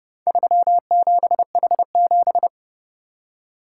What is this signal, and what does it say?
Morse code, heard as a keyed tone, about 27 wpm, 700 Hz. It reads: 37H7